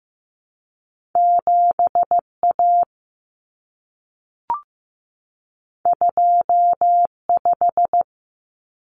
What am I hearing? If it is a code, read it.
Morse code, heard as a keyed tone, about 15 wpm, 700 Hz. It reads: 7A 25
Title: Common Cut Numbers - 2-Digit 15wpm